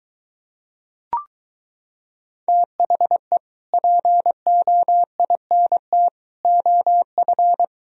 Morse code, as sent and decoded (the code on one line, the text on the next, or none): - .... / .--. --- .. -. - / --- ..-.
THE POINT OF